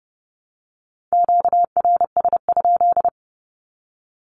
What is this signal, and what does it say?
Morse code, heard as a keyed tone, about 30 wpm, 700 Hz. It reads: QRS?
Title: Q-Codes 30wpm